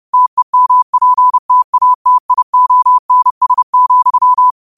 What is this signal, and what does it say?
Morse code, heard as a keyed tone, about 30 wpm, 1000 Hz. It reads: TEMPTATIONS,
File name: qotdfiles / 30wpm